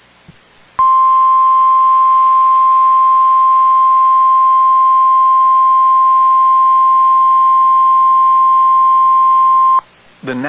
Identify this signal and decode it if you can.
Emergency Alert System (SAME) tones (the 1050 Hz attention tone). It is heard from 0.8 to 9.9 s.